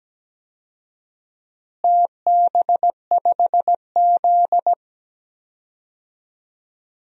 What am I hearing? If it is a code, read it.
Morse code, heard as a keyed tone, about 17 wpm, 700 Hz. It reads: TB5Z
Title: DXCC-100 Call Signs - 4 Characters - Repeated 2x - 17wpm